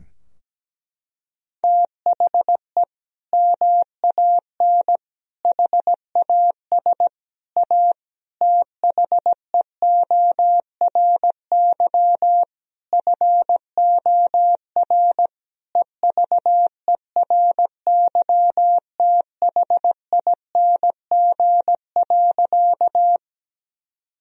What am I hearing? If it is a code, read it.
Morse code, heard as a keyed tone, about 17 wpm, 700 Hz. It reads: THE MAN HAS A THEORY FOR EVERYTHING.
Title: Sentences from Top 1000 Words - Review Part 3 - 17wpm